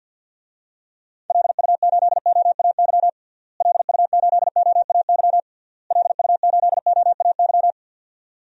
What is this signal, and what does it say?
Morse code, heard as a keyed tone, about 47 wpm, 700 Hz. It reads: LU8OAY LU8OAY LU8OAY
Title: DXCC-100 Call Signs - 6 Characters - Repeated 3x - 50wpm